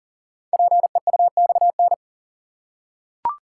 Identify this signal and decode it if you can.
Morse code, heard as a keyed tone, about 38 wpm, 700 Hz. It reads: PEUXN
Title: Code Groups - 5 Letters - 40wpm